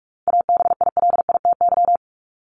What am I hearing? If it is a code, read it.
Morse code, heard as a keyed tone, about 44 wpm, 700 Hz. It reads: ABILITY
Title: Top 501-750 Words 45wpm